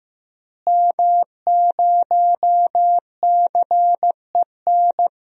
Morse code, heard as a keyed tone, about 15 wpm, 700 Hz. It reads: M0CEN
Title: DXCC-100 Call Signs - 5 Characters - Repeated 2x - 15wpm